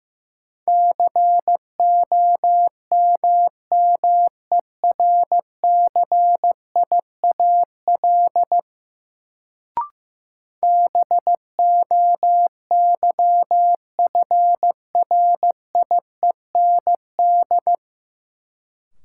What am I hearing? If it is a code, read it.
Morse code, heard as a keyed tone, about 15 wpm, 700 Hz. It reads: COMMERCIAL BOYFRIEND
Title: Sentences from Top 1000 Words - Review Part 5 - New Word Drill - 15wpm